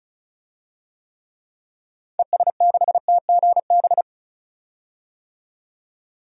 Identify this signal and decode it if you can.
Morse code, heard as a keyed tone, about 34 wpm, 700 Hz. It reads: ES6TGB
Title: DXCC-100 Call Signs - 6 Characters - 35wpm